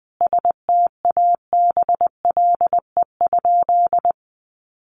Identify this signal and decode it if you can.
Morse code, heard as a keyed tone, about 20 wpm, 700 Hz. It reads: STABLE?
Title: Sentences from Top 1500 Words 20wpm